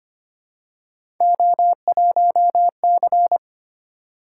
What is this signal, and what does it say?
Morse code, heard as a keyed tone, about 25 wpm, 700 Hz. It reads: O1C
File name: Three Letter-Number Combinations - 25wpm